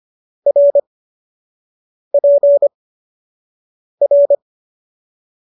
Morse code, heard as a keyed tone, about 25 wpm, spaced as for 6 wpm, 570 Hz. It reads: RPR